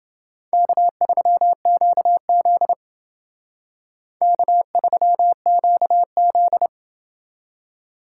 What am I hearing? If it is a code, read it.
Morse code, heard as a keyed tone, about 29 wpm, 700 Hz. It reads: K3QZ K3QZ